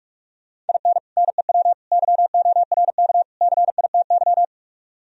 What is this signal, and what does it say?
Morse code, heard as a keyed tone, about 45 wpm, 700 Hz. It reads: IN NEW YORK CITY
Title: Speed-Racing - Sets of 4 Words 45wpm